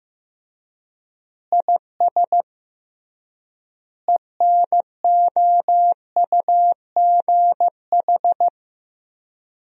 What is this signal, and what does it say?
Morse code, heard as a keyed tone, about 15 wpm, 700 Hz. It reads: IS ENOUGH
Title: 3x Word-Spacing - Sentences from Top 400 Words - 15wpm